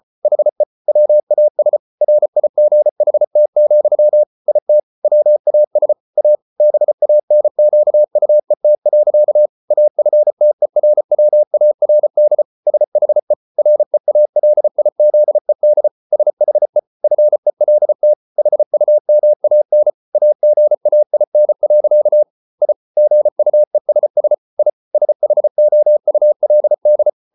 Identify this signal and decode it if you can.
Morse code, heard as a keyed tone, about 34 wpm, 600 Hz. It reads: HE WAS RIGHT, IT WAS A BANQUET. AFTERWARD SHE REALIZED SHE FELT HUMAN AGAIN. I GUESS I SHOULD